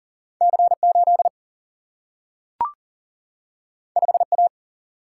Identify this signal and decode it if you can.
Morse code, heard as a keyed tone, about 39 wpm, 700 Hz. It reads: C 8 5 A
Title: Single-Letter Single-Number 40wpm